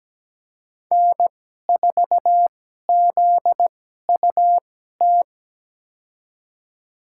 Morse code, heard as a keyed tone, about 17 wpm, spaced as for 12 wpm, 700 Hz. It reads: N4ZUT
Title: USA Call Signs - 5 Characters - 2x Character Spacing - 17wpm